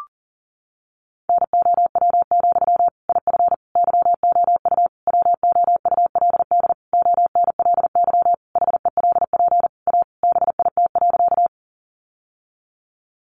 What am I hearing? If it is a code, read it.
Morse code, heard as a keyed tone, about 40 wpm, 700 Hz. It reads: NOW, IF YOU WOULD ONLY HELP A BIT.
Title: Sentences from Top 900 Words 40wpm